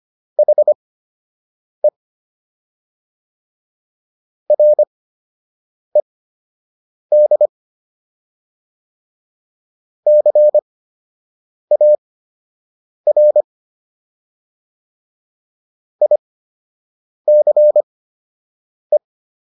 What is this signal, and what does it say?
Morse code, heard as a keyed tone, about 25 wpm, spaced as for 7 wpm, 600 Hz. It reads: HE RED CAR ICE